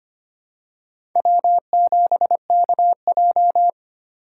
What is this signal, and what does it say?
Morse code, heard as a keyed tone, about 25 wpm, 700 Hz. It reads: W7KJ
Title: USA Call Signs - 3x Character Spacing - 25wpm